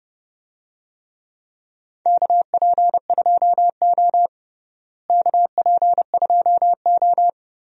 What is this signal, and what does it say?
Morse code, heard as a keyed tone, about 30 wpm, 700 Hz. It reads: KP2O KP2O
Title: DXCC-100 Call Signs - 4 Characters - Repeated 3x - 30wpm